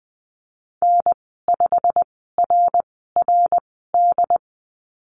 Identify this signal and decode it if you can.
Morse code, heard as a keyed tone, about 20 wpm, spaced as for 15 wpm, 700 Hz. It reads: N5RRD
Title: 128 Lesson 21g - Number 6 - Call Signs 20wpm